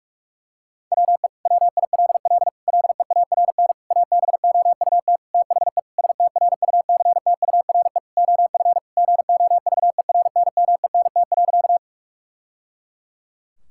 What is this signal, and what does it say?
Morse code, heard as a keyed tone, about 46 wpm, 700 Hz. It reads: WE WILL LEARN ABOUT THE STRUCTURE OF GOVERNMENT.
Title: Sentences from Top 1000 Words - Review Part 6 - 45wpm